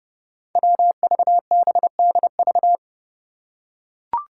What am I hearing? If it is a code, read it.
Morse code, heard as a keyed tone, about 32 wpm, 700 Hz. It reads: WVBDV
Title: Five Challenging Letter-Number Combinations 30wpm